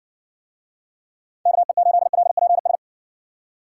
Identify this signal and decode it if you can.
Morse code, heard as a keyed tone, about 60 wpm, 700 Hz. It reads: KE8LPS